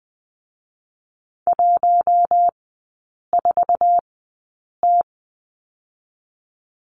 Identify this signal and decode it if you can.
Morse code, heard as a keyed tone, about 20 wpm, 700 Hz. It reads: 1 4 T
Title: 047 Lesson 9e - Number 4 - Three Characters 20wpm